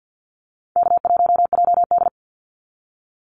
Morse code, heard as a keyed tone, about 48 wpm, 700 Hz. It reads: K1JD
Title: NA SOTA Activator Callsigns 50wpm